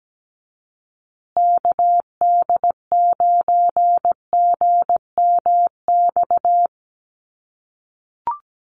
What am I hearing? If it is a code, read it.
Morse code, heard as a keyed tone, about 17 wpm, 700 Hz. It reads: KD9GMX
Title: USA Call Signs Repeated 3x 17wpm